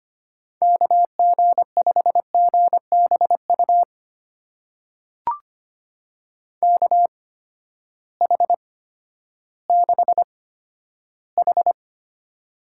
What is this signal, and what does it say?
Morse code, heard as a keyed tone, about 25 wpm, spaced as for 12 wpm, 700 Hz. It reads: KG5GBU KH6H
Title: USA Call Signs - 8x Character Spacing - 25wpm